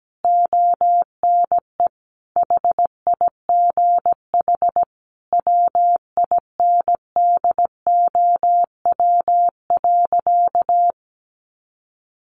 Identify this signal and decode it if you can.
Morse code, heard as a keyed tone, about 17 wpm, 700 Hz. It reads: ONE HIGH WINDOW.
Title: A Sound Of Thunder 17wpm